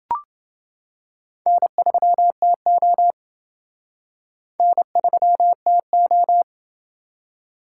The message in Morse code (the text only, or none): N3TO N3TO